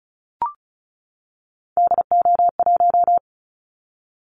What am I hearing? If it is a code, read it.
Morse code, heard as a keyed tone, about 36 wpm, 700 Hz. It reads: DO1
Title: Callsign Prefixes for Top 50 DXCC - 35wpm